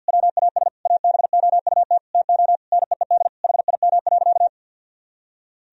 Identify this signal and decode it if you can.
Morse code, heard as a keyed tone, about 52 wpm, 700 Hz. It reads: WAS ABOUT TO NEED HIM.
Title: Sentences from Top 1000 Words - Review Part 1 - 50wpm